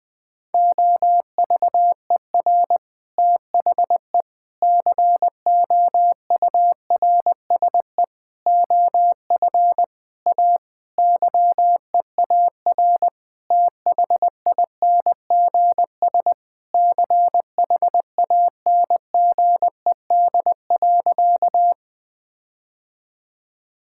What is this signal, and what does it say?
Morse code, heard as a keyed tone, about 20 wpm, 700 Hz. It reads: OVER THE COURSE OF A YEAR THINGS CHANGED.
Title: Sentences with 4-Word Phrases - 20wpm